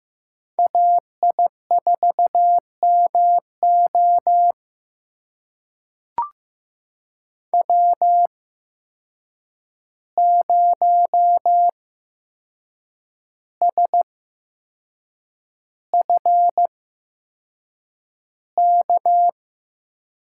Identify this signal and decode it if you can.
Morse code, heard as a keyed tone, about 15 wpm, spaced as for 6 wpm, 700 Hz. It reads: AI4MO W0SFK
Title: USA Call Signs - 5 Characters - 8x Character Spacing - 15wpm